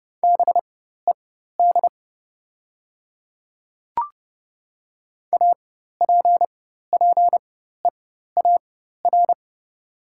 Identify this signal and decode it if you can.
Morse code, heard as a keyed tone, about 30 wpm, spaced as for 14 wpm, 700 Hz. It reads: BED APPEAR
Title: 4x Character Spacing - Top 500 Words - 30wpm